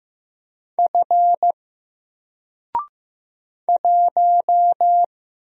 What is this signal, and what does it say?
Morse code, heard as a keyed tone, about 15 wpm, 700 Hz. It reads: F 1
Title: Single Letter-Number Alt 15wpm